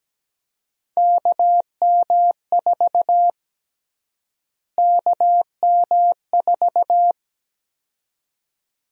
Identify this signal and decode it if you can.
Morse code, heard as a keyed tone, about 17 wpm, 700 Hz. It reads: KM4 KM4
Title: USA Call Sign Prefixes - Repeated 2x - 17wpm